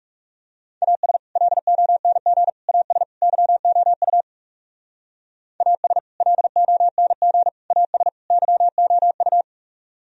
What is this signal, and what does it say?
Morse code, heard as a keyed tone, about 36 wpm, 690 Hz. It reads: AS LONG AS YOU AS LONG AS YOU
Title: Speed-Racing - Sets of 4 Words 30wpm